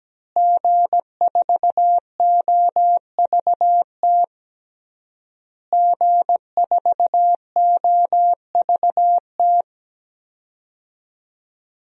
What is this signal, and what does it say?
Morse code, heard as a keyed tone, about 17 wpm, 700 Hz. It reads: G4OVT G4OVT
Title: DXCC-100 Call Signs - 5 Characters - Repeated 3x - 17wpm